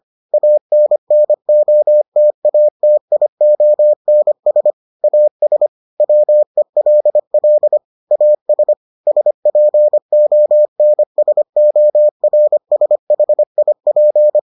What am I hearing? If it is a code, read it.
Morse code, heard as a keyed tone, about 25 wpm, 600 Hz. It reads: ANNOTATIONS AS WELL AS SPONSORSHIP